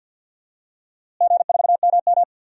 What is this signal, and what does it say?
Morse code, heard as a keyed tone, about 52 wpm, 700 Hz. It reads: G4MM
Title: DXCC-100 Call Signs - 4 Characters - Repeated 2x - 50wpm